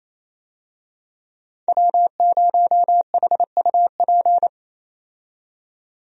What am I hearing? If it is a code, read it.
Morse code, heard as a keyed tone, about 28 wpm, 700 Hz. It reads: W0HUP
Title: USA Call Signs - 5 Characters - 28wpm